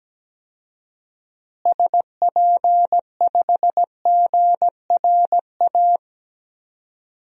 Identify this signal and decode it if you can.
Morse code, heard as a keyed tone, about 17 wpm, 700 Hz. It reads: SP5GRA
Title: DXCC-100 Call Signs - 6 Characters - Repeated 2x - 17wpm